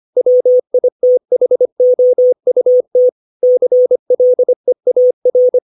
Morse code, heard as a keyed tone, about 25 wpm, 500 Hz. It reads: WITHOUT CLEAR
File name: 632 words / 632 Words 25WPM